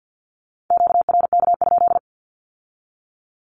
Morse code, heard as a keyed tone, about 50 wpm, 700 Hz. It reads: QRK?
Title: Q-Codes with Spoken Meaning - 50wpm